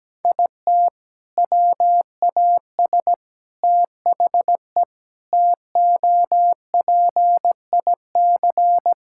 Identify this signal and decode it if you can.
Morse code, heard as a keyed tone, about 17 wpm, 700 Hz. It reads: IT WAS THE TOPIC
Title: Sentences from Top 1600 Words 17wpm